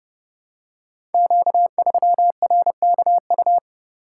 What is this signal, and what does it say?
Morse code, heard as a keyed tone, about 31 wpm, 700 Hz. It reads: Q3RKU
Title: Code Groups - 5 Letter:Numbers - 30wpm